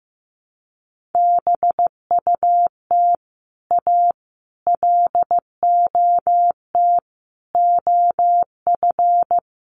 Morse code, heard as a keyed tone, about 15 wpm, 700 Hz. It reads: BUT A LOT OF
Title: Sets of 5 Words Encore2 15wpm